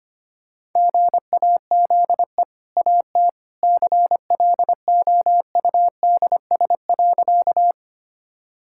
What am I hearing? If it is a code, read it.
Morse code, heard as a keyed tone, about 25 wpm, 700 Hz. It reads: GAZE AT CLOUDS.
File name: Easy Sentences with Hints - 25wpm